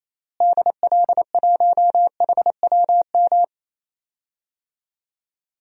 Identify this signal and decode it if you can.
Morse code, heard as a keyed tone, about 28 wpm, 700 Hz. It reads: DL1HWM
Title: DXCC-100 Call Signs - 6 Characters - Repeated 3x - 28wpm